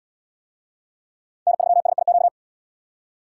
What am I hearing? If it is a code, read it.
Morse code, heard as a keyed tone, about 71 wpm, 700 Hz. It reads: N3SIQ